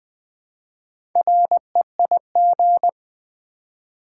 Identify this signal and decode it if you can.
Morse code, heard as a keyed tone, about 20 wpm, 700 Hz. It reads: REIG
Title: Build A Word - 1 Letter At A Time - 5-Letter Words - 20wpm